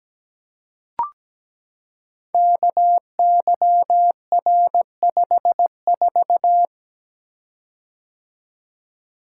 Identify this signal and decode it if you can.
Morse code, heard as a keyed tone, about 17 wpm, 700 Hz. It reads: KYR54
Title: Code Groups - 5 Letter:Numbers - 17wpm